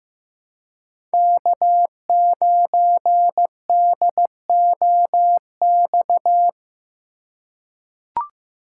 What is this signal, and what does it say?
Morse code, heard as a keyed tone, about 15 wpm, 700 Hz. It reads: K9DOX
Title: USA Call Signs - 5 Characters - 4x Character Spacing - 15wpm